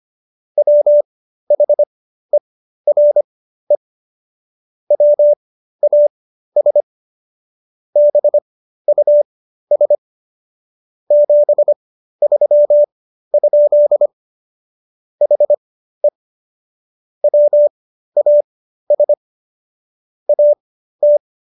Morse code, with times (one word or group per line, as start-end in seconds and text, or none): 0.6-3.8 WHERE
4.9-6.8 WAS
8.0-9.9 BUS
11.1-14.1 73?
15.2-16.1 HE
17.2-19.1 WAS
20.3-21.2 AT